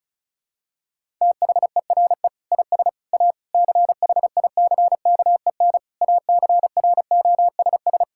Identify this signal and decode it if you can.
Morse code, heard as a keyed tone, about 34 wpm, 700 Hz. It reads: THERE IS A CHICKEN ACROSS
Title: Sentences from Top 1000 Words - Review Part 5 - 35wpm